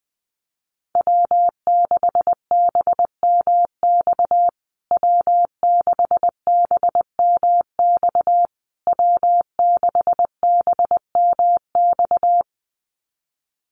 Morse code, heard as a keyed tone, about 20 wpm, 700 Hz. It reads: W6BMX W6BMX W6BMX
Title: USA Call Signs Repeated 3x 20wpm